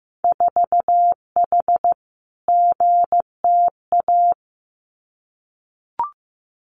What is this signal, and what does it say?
Morse code, heard as a keyed tone, about 15 wpm, 700 Hz. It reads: 4H GTA
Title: Winter Field Day Exchanges Encore 15wpm